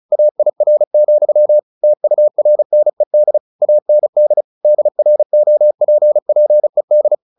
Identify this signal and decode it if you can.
Morse code, heard as a keyed tone, about 34 wpm, 600 Hz. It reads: AIR, TURNED AND DROPPED
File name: the old man and the sea / the old man and the sea 1 35wpm